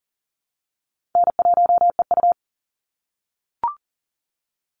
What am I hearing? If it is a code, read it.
Morse code, heard as a keyed tone, about 40 wpm, 700 Hz. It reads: N1EU
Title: NA SOTA Activator Callsigns 40wpm